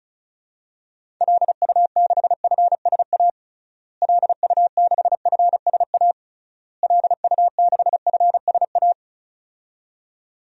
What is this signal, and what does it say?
Morse code, heard as a keyed tone, about 36 wpm, 700 Hz. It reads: LU6FSA LU6FSA LU6FSA
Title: DXCC-100 Call Signs - 6 Characters - Repeated 3x - 35wpm